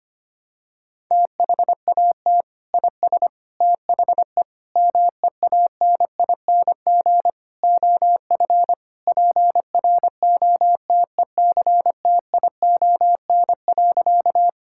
Morse code, heard as a keyed tone, about 25 wpm, 700 Hz. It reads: THAT IS THE MEANING OF PROTECTION.
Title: Sentences from Top 1600 Words 25wpm